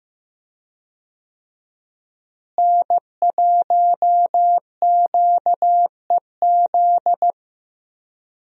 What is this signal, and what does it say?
Morse code, heard as a keyed tone, about 15 wpm, 700 Hz. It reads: N1QEZ
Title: USA Call Signs - 5 Characters - Repeated 3x - 15wpm